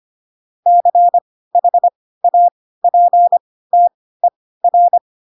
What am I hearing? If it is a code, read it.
Morse code, heard as a keyed tone, about 25 wpm, spaced as for 16 wpm, 700 Hz. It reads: CHAPTER
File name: Pinocchio / Pinocchio ch4 10 25 ik1hge